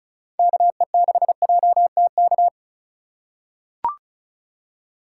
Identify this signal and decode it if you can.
Morse code, heard as a keyed tone, about 36 wpm, 700 Hz. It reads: KE6JTK
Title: USA Call Signs - 2x Character Spacing - 35wpm